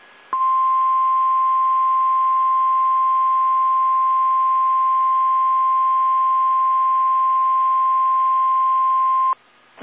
Emergency Alert System (SAME) tones (the 1050 Hz attention tone). It is heard from 0.3 to 9.4 s.
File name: eas / KSDZKDJL / KSDZKDJL